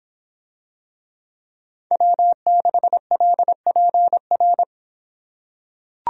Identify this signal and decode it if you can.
Morse code, heard as a keyed tone, about 25 wpm, 700 Hz. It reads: W6LPR